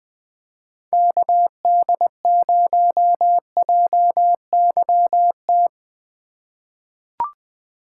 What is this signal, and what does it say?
Morse code, heard as a keyed tone, about 20 wpm, 700 Hz. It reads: KD0JYT
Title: USA Call Signs - 6 Characters - 2x Character Spacing - 20wpm